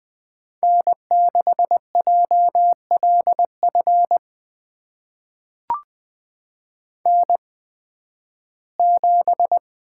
Morse code, heard as a keyed tone, about 20 wpm, spaced as for 13 wpm, 700 Hz. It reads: N6JLF N7
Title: USA Call Signs - 5 Characters - 8x Character Spacing - 20wpm